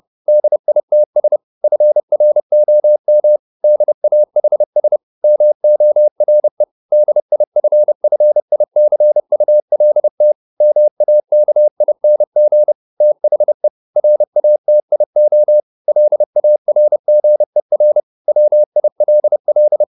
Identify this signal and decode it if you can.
Morse code, heard as a keyed tone, about 30 wpm, 600 Hz. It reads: DITS FROM DAHS MORE DIFFICULT MAKING THE RATIO LARGER WILL